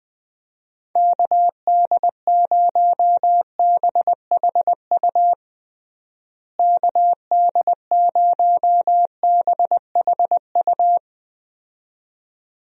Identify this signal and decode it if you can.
Morse code, heard as a keyed tone, about 20 wpm, 700 Hz. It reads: KD0BHU KD0BHU
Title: USA Call Signs - 6 Characters - Repeated 2x - 20wpm